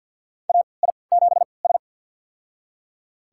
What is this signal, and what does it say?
Morse code, heard as a keyed tone, about 48 wpm, spaced as for 28 wpm, 700 Hz. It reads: AI7S
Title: USA Call Signs - 4 Characters - 3x Character Spacing - 50wpm